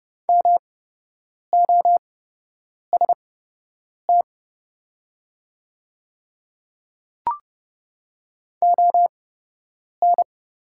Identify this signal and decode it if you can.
Morse code, heard as a keyed tone, about 30 wpm, spaced as for 8 wpm, 690 Hz. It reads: MOST ON
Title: 8x Character Spacing - Top 200 Words - 30wpm